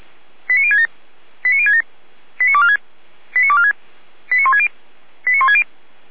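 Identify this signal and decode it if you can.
Five-tone selective call (ZVEI), heard as quick runs of five tones, tones of about 70 ms. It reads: 89067 89067 89246 89246 89160 89160
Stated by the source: ZVEI five-tone selective call received by SDR.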